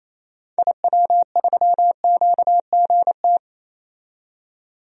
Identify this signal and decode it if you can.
Morse code, heard as a keyed tone, about 28 wpm, 700 Hz. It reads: IW3QGT
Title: DXCC-100 Call Signs - 6 Characters - 28wpm